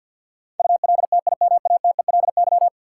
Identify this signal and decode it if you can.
Morse code, heard as a keyed tone, about 51 wpm, 700 Hz. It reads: ULTIMATELY
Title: Sentences from Top 1200 Words - New Word Drill - 50wpm